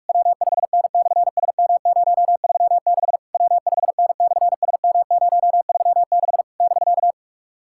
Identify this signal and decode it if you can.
Morse code, heard as a keyed tone, about 46 wpm, 700 Hz. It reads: W5N/SM036 W5N/SM036 <BK>
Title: SOTA QSO Practice 45wpm